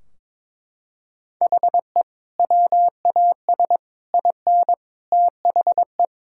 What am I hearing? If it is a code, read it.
Morse code, heard as a keyed tone, about 22 wpm, 700 Hz. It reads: HE WAS IN THE
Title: Sentences from Top 1600 Words 22wpm